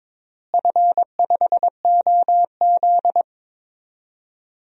Morse code, heard as a keyed tone, about 23 wpm, 700 Hz. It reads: F5OZ
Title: DXCC-100 Call Signs - 4 Characters - Repeated 2x - 22wpm